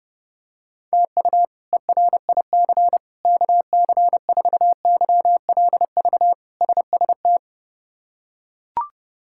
Morse code, heard as a keyed tone, about 30 wpm, 700 Hz. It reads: TU ERIC KC4YLV SST
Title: K1USN SST QSO Practice - 30wpm